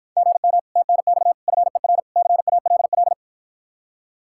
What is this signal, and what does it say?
Morse code, heard as a keyed tone, about 54 wpm, 700 Hz. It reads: GM TNX FER CALL